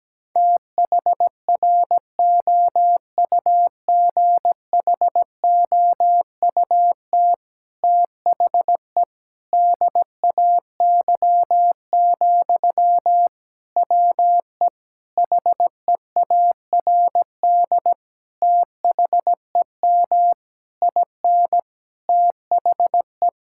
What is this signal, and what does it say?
Morse code, heard as a keyed tone, about 17 wpm, 700 Hz. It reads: THROUGHOUT THE DAY, WE HEARD THEM IN THE
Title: Sentences from Top 1000 Words - Review Part 5 - 17wpm